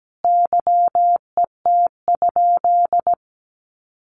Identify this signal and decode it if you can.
Morse code, heard as a keyed tone, about 17 wpm, 700 Hz. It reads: YET?
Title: Sentences from Top 1100 Words 17wpm